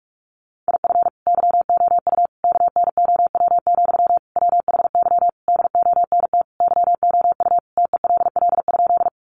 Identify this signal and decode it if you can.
Morse code, heard as a keyed tone, about 44 wpm, 700 Hz. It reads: IF YOU KNOW, WHY DONT YOU TELL?
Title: Sentences from Top 300 Words 45wpm